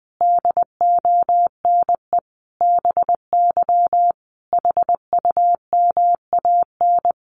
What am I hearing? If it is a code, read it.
Morse code, heard as a keyed tone, about 20 wpm, 700 Hz. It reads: DONE BY HUMAN
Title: Sentences from Top 1300 Words 20wpm